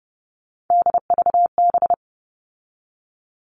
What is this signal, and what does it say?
Morse code, heard as a keyed tone, about 30 wpm, 700 Hz. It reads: DVB
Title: All Cut Numbers - 3-Digit 30wpm